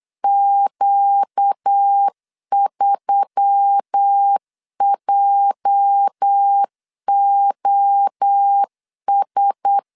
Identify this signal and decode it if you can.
Morse code, heard as a keyed tone, about 8 wpm, 800 Hz. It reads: Q3JOS
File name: LEZIONI / 8 YJ45 / 8 YJ 8wpm 3313